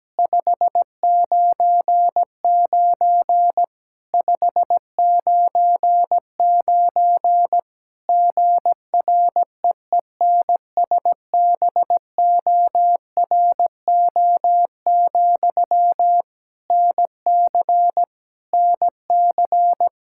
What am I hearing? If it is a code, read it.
Morse code, heard as a keyed tone, about 17 wpm, 700 Hz. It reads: 599 599 GREENSBORO, NC NC
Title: K1USN SST QSO Practice 17wpm